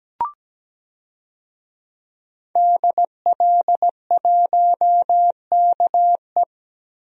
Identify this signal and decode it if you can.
Morse code, heard as a keyed tone, about 17 wpm, 700 Hz. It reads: DL1KE
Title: DXCC-100 Call Signs - 5 Characters - 17wpm